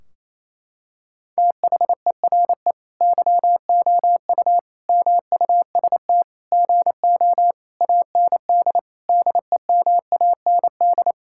Morse code, heard as a keyed tone, about 28 wpm, 700 Hz. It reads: THERE YOU MUST GO AND DEMAND
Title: Sentences from Top 1600 Words 28wpm